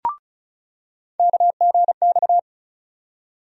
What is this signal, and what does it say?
Morse code, heard as a keyed tone, about 34 wpm, 700 Hz. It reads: KGX
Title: USA Call Sign Postfixes - 3 Letters 35wpm